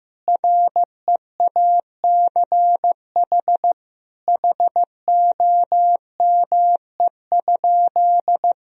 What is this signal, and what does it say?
Morse code, heard as a keyed tone, about 15 wpm, 700 Hz. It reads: REACH HOME?
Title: Sentences from Top 1000 Words - Review Part 2 - 15wpm